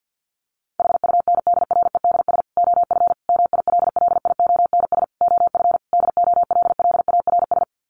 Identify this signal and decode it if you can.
Morse code, heard as a keyed tone, about 50 wpm, 700 Hz. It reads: HUNDREDS OF MILLIONS OF DOLLARS
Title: Sets of 5 Words 50wpm